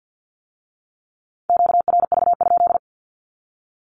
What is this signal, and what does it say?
Morse code, heard as a keyed tone, about 50 wpm, 700 Hz. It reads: QRV?
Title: Q-Codes with Spoken Meaning - 50wpm